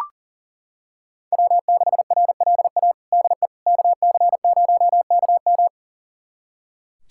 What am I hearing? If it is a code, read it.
Morse code, heard as a keyed tone, about 40 wpm, 700 Hz. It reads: W6RLA DE KC0KM
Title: USA Callsign DE Callsign - 5 Characters - 40wpm